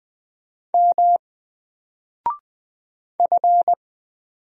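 Morse code, heard as a keyed tone, about 20 wpm, 700 Hz. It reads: M F
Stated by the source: Single letter and number Morse drill at 20 wpm, from N4LCD.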